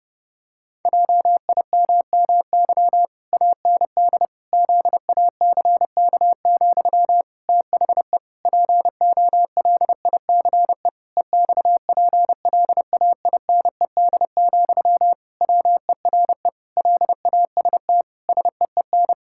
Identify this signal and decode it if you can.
Morse code, heard as a keyed tone, about 32 wpm, 700 Hz. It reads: JIMMY AND ZACK, THE POLICE EXPLAINED, WERE LAST SEEN
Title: Pangram Challenge v2 - 30wpm